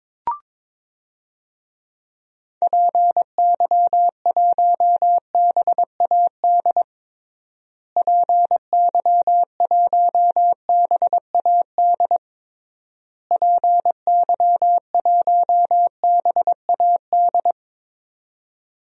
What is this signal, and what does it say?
Morse code, heard as a keyed tone, about 22 wpm, 700 Hz. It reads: PY1BAD PY1BAD PY1BAD
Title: DXCC-100 Call Signs - 6 Characters - Repeated 3x - 22wpm